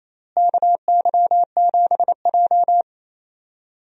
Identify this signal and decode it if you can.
Morse code, heard as a keyed tone, about 28 wpm, 700 Hz. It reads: KY7J
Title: USA Call Signs - 4 Characters - Repeated 3x - 28wpm